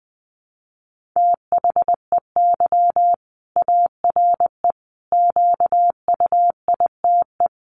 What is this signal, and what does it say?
Morse code, heard as a keyed tone, about 20 wpm, 700 Hz. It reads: THEY ARE QUITE
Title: Sentences from Top 700 Words 20wpm